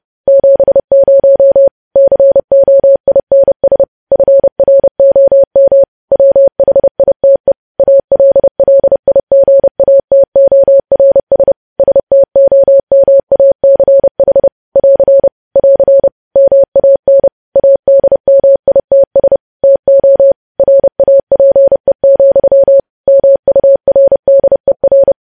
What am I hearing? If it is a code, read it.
Morse code, heard as a keyed tone, about 30 wpm, 550 Hz. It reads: 70 COINS FROM WHITE ALLIGATORS STOMACH <AR> <AR> MAN ADMITS TO RAPE, MURDER
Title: News Headlines at 30